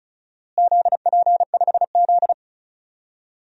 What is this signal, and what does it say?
Morse code, heard as a keyed tone, about 36 wpm, 700 Hz. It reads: ZP5Z
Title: DXCC-100 Call Signs - 4 Characters - Repeated 3x - 35wpm